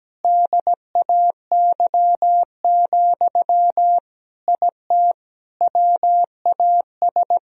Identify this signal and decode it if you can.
Morse code, heard as a keyed tone, about 17 wpm, 700 Hz. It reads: DAY, IT WAS
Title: Sentences from Top 1000 Words - Review Part 3 - 17wpm